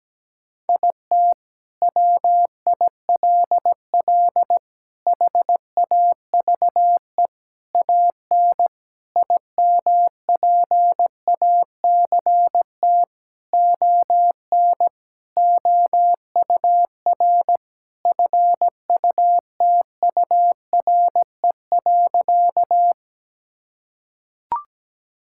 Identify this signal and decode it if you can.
Morse code, heard as a keyed tone, about 17 wpm, 700 Hz. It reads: IT WILL HAVE AN IMPACT ON OUR FUTURE.
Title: Sentences from Top 1000 Words - Review Part 6 - 17wpm